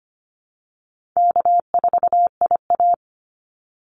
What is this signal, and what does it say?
Morse code, heard as a keyed tone, about 25 wpm, 700 Hz. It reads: K4IA
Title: Easy Call Signs Repeated 3x - 25wpm